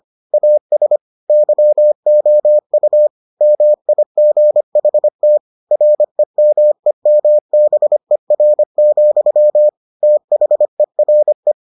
Morse code, heard as a keyed tone, about 25 wpm, 600 Hz. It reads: AS YOU MIGHT REMEMBER, THERE